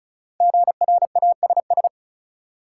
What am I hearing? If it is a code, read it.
Morse code, heard as a keyed tone, about 36 wpm, 700 Hz. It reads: GRASS
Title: Five Letter Words - 35wpm